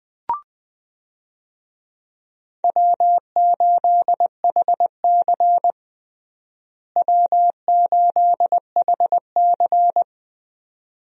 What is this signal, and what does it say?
Morse code, heard as a keyed tone, about 20 wpm, 700 Hz. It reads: W8HC W8HC
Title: USA Call Signs - 4 Characters - Repeated 2x - 20wpm